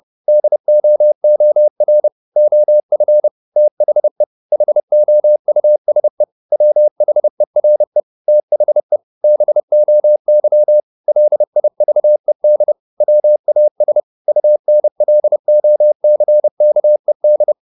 Morse code, heard as a keyed tone, about 30 wpm, 600 Hz. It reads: DOOR OF THE HOUSE WHERE THE BOY LIVED WAS UNLOCKED
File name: the old man and the sea / the old man and the sea 1 30wpm